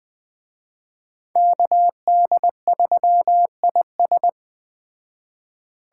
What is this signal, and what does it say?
Morse code, heard as a keyed tone, about 20 wpm, 700 Hz. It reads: KD3IS
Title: Speed Racing - USA 2x2 Call Signs 17wpm